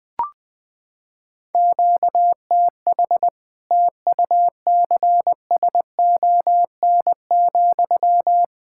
Morse code, heard as a keyed tone, about 21 wpm, 700 Hz. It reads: QTH TUCSON,
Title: Practice QSO - Exchange 1 - Beginning - 20wpm